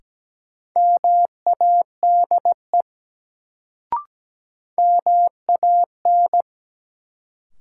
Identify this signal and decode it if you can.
Morse code, heard as a keyed tone, about 17 wpm, 700 Hz. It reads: MADE MAN
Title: Sentences from Top 200 Words - New Word Drill - 17wpm